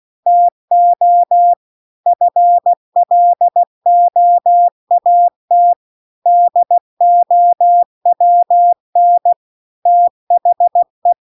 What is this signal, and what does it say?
Morse code, heard as a keyed tone, about 16 wpm, 700 Hz. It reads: TO FLOAT DOWN THE
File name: CW-The-Big-Twoheaded-River-1-16WPM